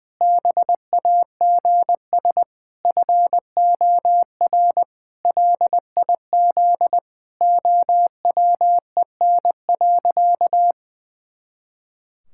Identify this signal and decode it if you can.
Morse code, heard as a keyed tone, about 20 wpm, 700 Hz. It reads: BAGS FOR LIZ OWEN.
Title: Pangram Challenge v2 - 20wpm